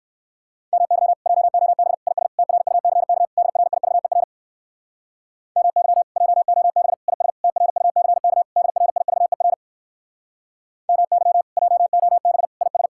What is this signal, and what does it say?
Morse code, heard as a keyed tone, about 59 wpm, 700 Hz. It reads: MY JOB IS TRUCK DRIVER MY JOB IS TRUCK DRIVER MY JOB IS